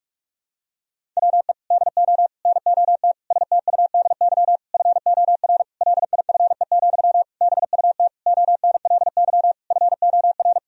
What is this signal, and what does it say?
Morse code, heard as a keyed tone, about 46 wpm, 700 Hz. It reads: WE DO NOT STUDY FOR LIFE, BUT ONLY FOR
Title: Sentences from Top 1000 Words - Review Part 1 - 45wpm